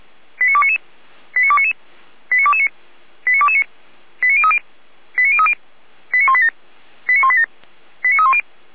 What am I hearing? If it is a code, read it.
Five-tone selective call (ZVEI), heard as quick runs of five tones, tones of about 80 ms. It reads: 89299 89299 89229 89229 89030 89030 89187 89187 89310
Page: SDR/ZVEI